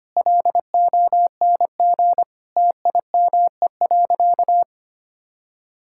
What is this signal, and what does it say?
Morse code, heard as a keyed tone, about 25 wpm, 700 Hz. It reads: LONG TIME.
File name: practice / 25wpm / Sentences with 3-Word Phrases - 25wpm